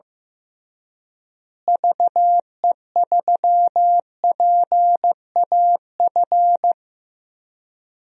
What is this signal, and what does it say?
Morse code, heard as a keyed tone, about 15 wpm, 700 Hz. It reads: VE3PAF
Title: DXCC-100 Call Signs - 6 Characters - Repeated 3x - 15wpm